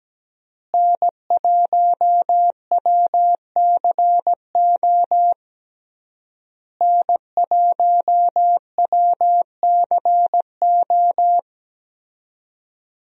Morse code, heard as a keyed tone, about 17 wpm, 700 Hz. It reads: N1WCO N1WCO
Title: USA Call Signs - 5 Characters - Repeated 3x - 17wpm